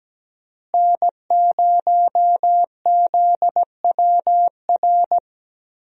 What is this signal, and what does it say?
Morse code, heard as a keyed tone, about 17 wpm, 700 Hz. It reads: N0ZWR
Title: USA Call Signs - 5 Characters - Repeated 3x - 17wpm